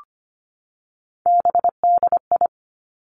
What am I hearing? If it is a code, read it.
Morse code, heard as a keyed tone, about 25 wpm, 700 Hz. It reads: BDI
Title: Three Challenging Letter-Number Combinations 25wpm